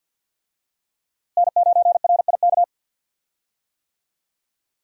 Morse code, heard as a keyed tone, about 50 wpm, 700 Hz. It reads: N9RIK
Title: USA Call Signs - 5 Characters - 50wpm